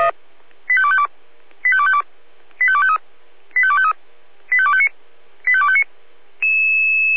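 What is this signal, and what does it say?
Five-tone selective call (ZVEI), heard as quick runs of five tones, tones of about 70 ms. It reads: 85252 85252 85253 85253 85259 85259
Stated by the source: ZVEI five-tone selective call, received on SDR.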